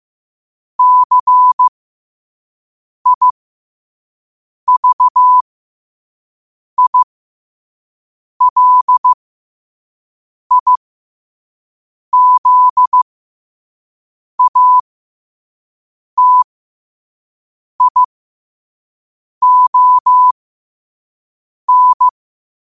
Morse code, heard as a keyed tone, about 15 wpm, spaced as for 5 wpm, 1000 Hz. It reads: CIVILIZATION
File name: qotdfiles / 5wpm